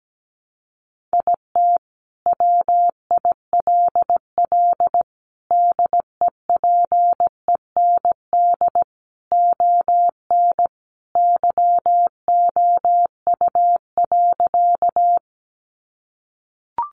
Morse code, heard as a keyed tone, about 17 wpm, 700 Hz. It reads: IT WILL DEPEND ON YOU.
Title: Sentences from Top 1300 Words 17wpm